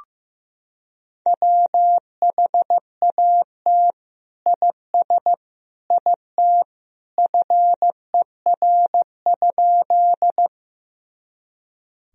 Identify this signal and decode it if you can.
Morse code, heard as a keyed tone, about 15 wpm, 700 Hz. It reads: WHAT IS IT FER?
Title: Sentences from Top 100 Words and Abbreviations 15wpm